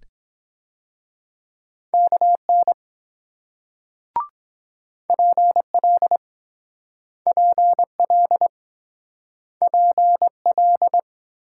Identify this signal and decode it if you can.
Morse code, heard as a keyed tone, about 23 wpm, 700 Hz. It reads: KN PL PL PL